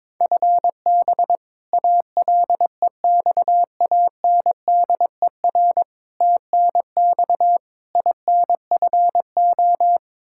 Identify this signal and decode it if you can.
Morse code, heard as a keyed tone, about 22 wpm, 700 Hz. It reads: FB ALEXANDER TNX INFO
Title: K1USN SST QSO Practice 22wpm